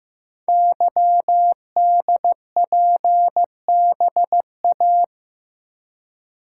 Morse code, heard as a keyed tone, about 15 wpm, 700 Hz. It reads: YDPBA
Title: Code Groups - 5 Letters - 15wpm